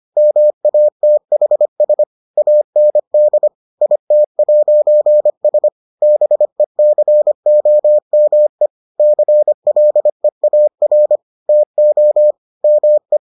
Morse code, heard as a keyed tone, about 25 wpm, 600 Hz. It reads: MATHS AND IT'S BECOME CLEAR TO ME